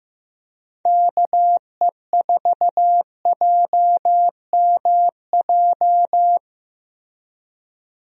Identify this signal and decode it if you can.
Morse code, heard as a keyed tone, about 15 wpm, 700 Hz. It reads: KE4JMJ
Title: USA Call Signs - 6 Characters - Repeated 2x - 15wpm